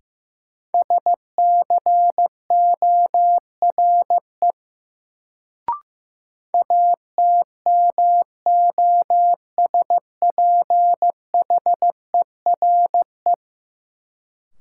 Morse code, heard as a keyed tone, about 15 wpm, 700 Hz. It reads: SCORE ATMOSPHERE
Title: Sentences from Top 1600 Words - New Word Drill - 15wpm